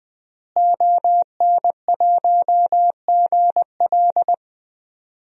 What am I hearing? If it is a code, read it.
Morse code, heard as a keyed tone, about 20 wpm, 700 Hz. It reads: ON1GL
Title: DXCC-100 Call Signs - 5 Characters - Repeated 2x - 20wpm